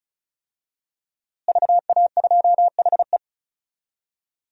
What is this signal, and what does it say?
Morse code, heard as a keyed tone, about 34 wpm, 700 Hz. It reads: VA2HE
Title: DXCC-100 Call Signs - 5 Characters - Repeated 2x - 35wpm